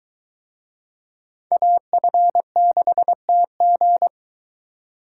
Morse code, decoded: AF6TG

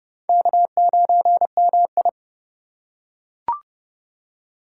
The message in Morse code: K9MI

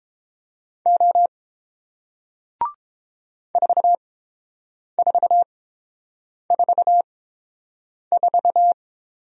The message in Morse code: O 4444

